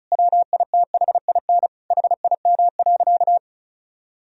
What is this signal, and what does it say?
Morse code, heard as a keyed tone, about 36 wpm, 700 Hz. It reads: WITHIN HIM.
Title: Sentences from Top 1000 Words - Review Part 4 - 35wpm